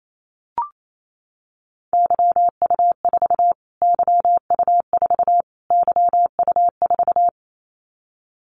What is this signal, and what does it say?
Morse code, heard as a keyed tone, about 28 wpm, 700 Hz. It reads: YU4 YU4 YU4
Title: Callsign Prefixes for Top 25 DXCC - Repeated 3x - 28wpm